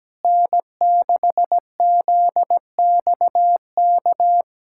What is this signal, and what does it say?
Morse code, heard as a keyed tone, about 17 wpm, 700 Hz. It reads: N6ZXK
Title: USA Callsign DE Callsign - 5 Characters - 17wpm